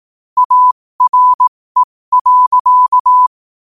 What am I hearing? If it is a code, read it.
Morse code, heard as a keyed tone, about 18 wpm, spaced as for 15 wpm, 1000 Hz. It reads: ARE.